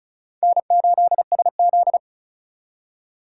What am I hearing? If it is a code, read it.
Morse code, heard as a keyed tone, about 36 wpm, 700 Hz. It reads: N8SZ
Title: USA Call Signs - 4 Characters - 2x Character Spacing - 35wpm